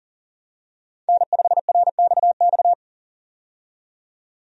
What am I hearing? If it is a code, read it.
Morse code, heard as a keyed tone, about 40 wpm, 700 Hz. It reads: N5RXX
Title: USA Call Signs - 5 Characters - 40wpm